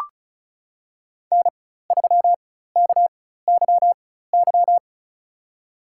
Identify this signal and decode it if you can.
Morse code, heard as a keyed tone, about 34 wpm, spaced as for 16 wpm, 700 Hz. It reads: N3KYY
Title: USA Call Signs - 4x Character Spacing - 35wpm